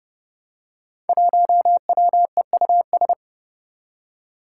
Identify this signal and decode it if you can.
Morse code, heard as a keyed tone, about 30 wpm, 700 Hz. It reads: 1WEUS
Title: Five Challenging Letter-Number Combinations 30wpm